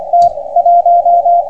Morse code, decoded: T 1